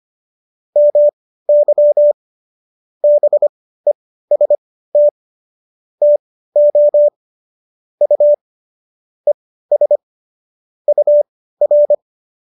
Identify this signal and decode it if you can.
Morse code, heard as a keyed tone, about 25 wpm, spaced as for 15 wpm, 590 Hz. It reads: MY BEST TO U ES UR